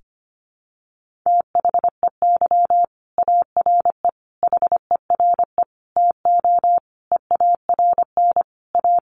Morse code, heard as a keyed tone, about 25 wpm, 700 Hz. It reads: THEY ARE HERE TO EARN A
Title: Sentences from Top 1500 Words 25wpm